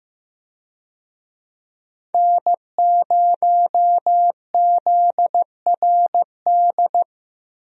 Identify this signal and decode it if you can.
Morse code, heard as a keyed tone, about 15 wpm, 700 Hz. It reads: N0ZRD
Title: USA Call Signs - 5 Characters - Repeated 2x - 15wpm